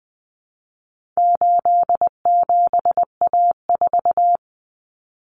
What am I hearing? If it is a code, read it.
Morse code, heard as a keyed tone, about 20 wpm, 700 Hz. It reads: 87A4
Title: Common Cut Numbers - 4-Digit 20wpm